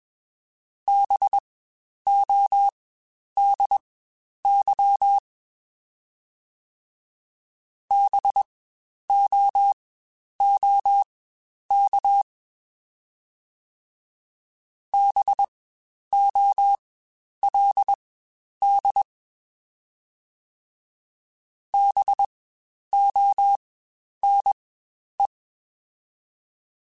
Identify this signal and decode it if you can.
Morse code, heard as a keyed tone, about 21 wpm, spaced as for 10 wpm, 800 Hz. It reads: BODY BOOK BOLD BONE
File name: Spaced Alphas / Spaced Alphas mp3s / Bs sp @20